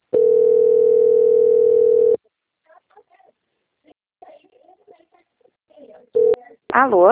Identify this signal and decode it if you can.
Ringback tone (North America), 440 + 480 Hz, with speech.
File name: out- - - - - .17811